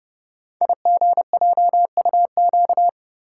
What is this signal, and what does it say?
Morse code, heard as a keyed tone, about 28 wpm, 690 Hz. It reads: IGJUQ
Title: Code Groups - 5 Letters - 30wpm